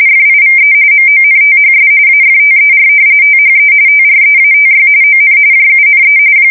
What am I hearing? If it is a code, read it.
RTTY, about 45.45 baud. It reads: FIVE MONTHS AFTER THE TREATY ON JUNE
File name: fsk170 / usnavy2